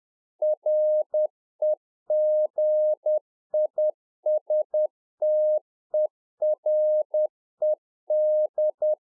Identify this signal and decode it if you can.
Morse code, heard as a keyed tone, about 10 wpm, 610 Hz. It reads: REGISTERED
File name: UNFOGETTABLE QSO 10WPM